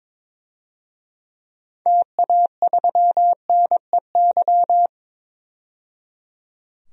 Morse code, heard as a keyed tone, about 22 wpm, 700 Hz. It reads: TA3NEY
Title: DXCC-100 Call Signs - 6 Characters - 22wpm